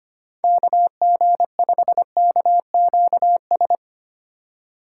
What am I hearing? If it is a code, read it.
Morse code, heard as a keyed tone, about 25 wpm, 700 Hz. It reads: KG5KQS
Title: USA Call Signs - 6x Character Spacing - 25wpm